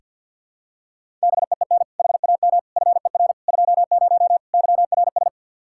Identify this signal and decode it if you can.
Morse code, heard as a keyed tone, about 46 wpm, 700 Hz. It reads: BEEN HAM FER 20 YRS
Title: Practice QSO - Exchange 3 - More Info - 50wpm